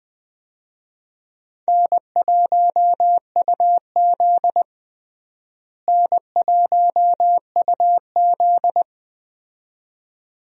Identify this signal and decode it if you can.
Morse code, heard as a keyed tone, about 20 wpm, 700 Hz. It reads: N1UZ N1UZ
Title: USA Call Signs - 4 Characters - Repeated 2x - 20wpm